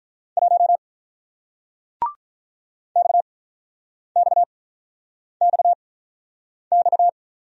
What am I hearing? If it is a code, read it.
Morse code, heard as a keyed tone, about 46 wpm, spaced as for 9 wpm, 700 Hz. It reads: 1 XXXX